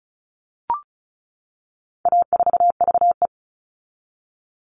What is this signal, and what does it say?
Morse code, heard as a keyed tone, about 34 wpm, 700 Hz. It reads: A4VE
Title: All Cut Numbers - 4-Digit 35wpm